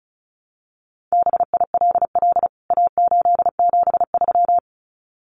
Morse code, heard as a keyed tone, about 34 wpm, 700 Hz. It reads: BILL A873
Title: CWT Exchange 35wpm